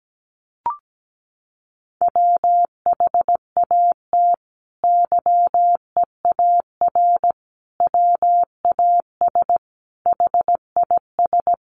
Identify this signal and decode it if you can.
Morse code, heard as a keyed tone, about 17 wpm, 700 Hz. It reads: WHAT YEAR WAS HIS
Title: Sentences from Top 800 Words 17wpm